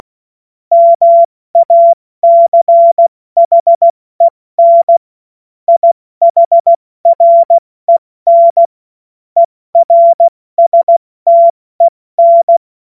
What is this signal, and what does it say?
Morse code, heard as a keyed tone, about 16 wpm, 680 Hz. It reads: MACHEN IHREN ERSTEN
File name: Morsekurs / Witze / Witz 07. Dezember 2021, 70 F80